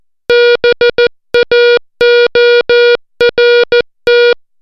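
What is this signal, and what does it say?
Morse code, heard as a keyed tone, about 14 wpm, 480 Hz. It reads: BAORT
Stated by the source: Morse alphabet lesson, 60 characters per minute.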